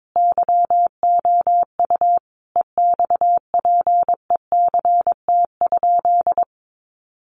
Morse code, heard as a keyed tone, about 22 wpm, 700 Hz. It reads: YOU EXPECT?
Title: Sentences from Top 1100 Words 22wpm